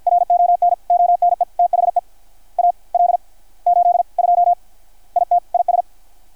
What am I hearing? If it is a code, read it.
Morse code, heard as a keyed tone, about 53 wpm, spaced as for 39 wpm, 690 Hz. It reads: RON ONE THE AL82 IT IS
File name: archived / CRSnet